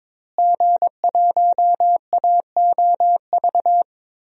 Morse code, heard as a keyed tone, about 22 wpm, 700 Hz. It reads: G1AOV
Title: DXCC-100 Call Signs - 5 Characters - Repeated 3x - 22wpm